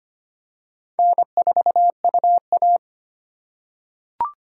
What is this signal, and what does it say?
Morse code, heard as a keyed tone, about 25 wpm, 700 Hz. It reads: N4UA
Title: USA Call Signs - 4 Characters - 3x Character Spacing - 25wpm